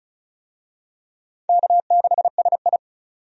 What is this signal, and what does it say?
Morse code, heard as a keyed tone, about 36 wpm, 700 Hz. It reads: K6SI
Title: USA Call Signs - 4 Characters - 35wpm